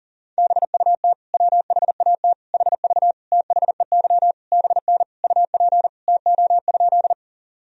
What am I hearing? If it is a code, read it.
Morse code, heard as a keyed tone, about 38 wpm, 700 Hz. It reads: BUT WHAT HV THEY BN UP TO?
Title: Sentences from Top 100 Words and Abbreviations 40wpm